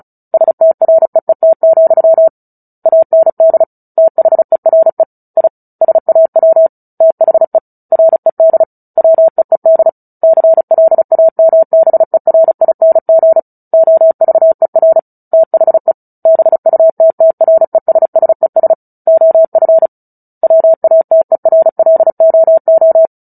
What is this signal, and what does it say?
Morse code, heard as a keyed tone, about 34 wpm, 650 Hz. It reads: STREET, AND THERE I SAW THE RED WEED CLAMBERING OVER THE BUTTRESSES OF WATERLOO
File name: Book Two / 09-Chapter 9 - 35 WPM